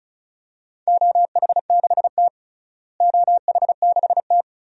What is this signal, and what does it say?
Morse code, heard as a keyed tone, about 34 wpm, 700 Hz. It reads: OH6T OH6T
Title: DXCC-100 Call Signs - 4 Characters - Repeated 3x - 35wpm